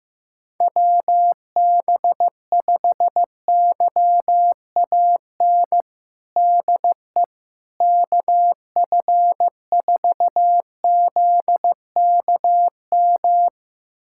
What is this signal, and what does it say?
Morse code, heard as a keyed tone, about 16 wpm, 700 Hz. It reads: WB5YAN DE KF4ZKM
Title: USA Callsign DE Callsign 15wpm